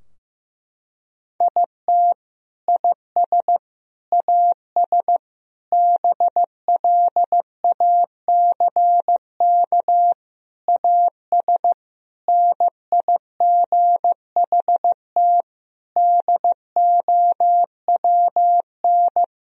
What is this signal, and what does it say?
Morse code, heard as a keyed tone, about 15 wpm, 700 Hz. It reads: IT IS AS BLACK AS NIGHT DOWN
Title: Sentences from Top 1000 Words - Review Part 2 - 15wpm